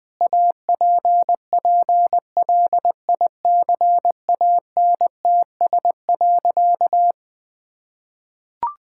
Morse code, heard as a keyed tone, about 20 wpm, 700 Hz. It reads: APPLICANTS.
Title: Sentences with 6-Word Phrases - Spoken Beginning - 20wpm